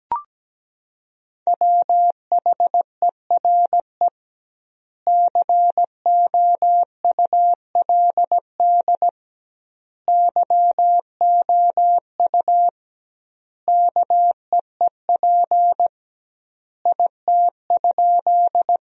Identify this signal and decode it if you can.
Morse code, heard as a keyed tone, about 17 wpm, 700 Hz. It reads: WHERE COULD YOU KEEP IT?
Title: 2x Word-Spacing - Sentences from Top 300 Words - 17wpm